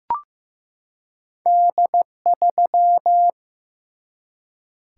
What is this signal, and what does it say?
Morse code, heard as a keyed tone, about 15 wpm, 700 Hz. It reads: D3
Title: Single-Letter Single-Number 15wpm